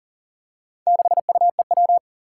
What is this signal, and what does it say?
Morse code, heard as a keyed tone, about 40 wpm, 700 Hz. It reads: 6UEW
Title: Four Challenging Letter-Number Combinations 40wpm